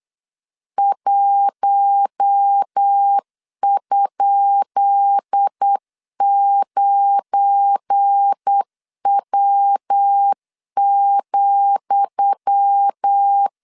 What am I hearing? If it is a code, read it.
Morse code, heard as a keyed tone, about 8 wpm, 800 Hz. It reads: 1?9W,
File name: LEZIONI / 10 SIGNS / 10D COMA 8wpm 3749